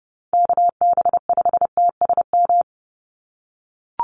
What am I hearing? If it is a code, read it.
Morse code, heard as a keyed tone, about 30 wpm, 700 Hz. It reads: KB5TSM
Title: USA Call Signs 30wpm